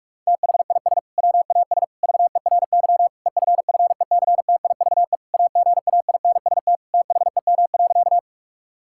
Morse code, heard as a keyed tone, about 46 wpm, 700 Hz. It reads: THIS WAS VERY EFFECTIVE AGAINST THEM.
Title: Sentences from Top 1000 Words - Review Part 6 - 45wpm